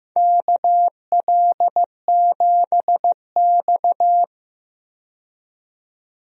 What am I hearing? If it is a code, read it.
Morse code, heard as a keyed tone, about 15 wpm, 700 Hz. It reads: KL7X
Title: DXCC-100 Call Signs - 4 Characters - 15wpm